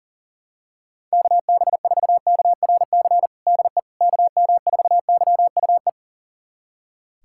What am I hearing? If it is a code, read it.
Morse code, heard as a keyed tone, about 40 wpm, 700 Hz. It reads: KB4KRC DE KM4YUE
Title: USA Callsign DE Callsign - 6 Characters - 40wpm